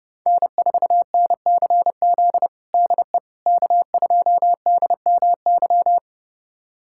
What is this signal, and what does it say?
Morse code, heard as a keyed tone, about 32 wpm, 700 Hz. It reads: N4NCZ DE K2DMY
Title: USA Callsign DE Callsign - 5 Characters - 30wpm